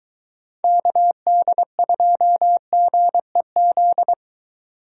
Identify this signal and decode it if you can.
Morse code, heard as a keyed tone, about 22 wpm, 700 Hz. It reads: KD2GEZ